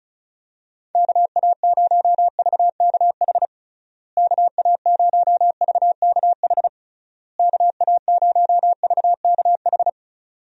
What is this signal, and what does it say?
Morse code, heard as a keyed tone, about 34 wpm, 700 Hz. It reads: KA0VKH KA0VKH KA0VKH
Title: USA Call Signs - 6 Characters - Repeated 3x - 35wpm